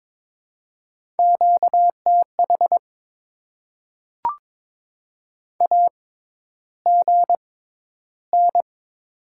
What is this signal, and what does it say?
Morse code, heard as a keyed tone, about 22 wpm, spaced as for 11 wpm, 700 Hz. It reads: QTH AGN
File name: Top 20 QSO Elements - 6x Character Spacing - 22wpm